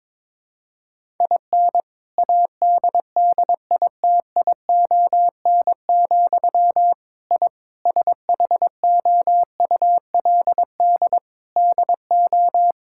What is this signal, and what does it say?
Morse code, heard as a keyed tone, about 22 wpm, 700 Hz. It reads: IN ADDITION, I SHOULD DO
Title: Sentences from Top 1000 Words - Review Part 6 - 22wpm